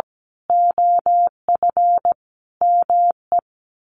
Morse code, heard as a keyed tone, about 17 wpm, 700 Hz. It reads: OF ME
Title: Sets of 4 Words 17wpm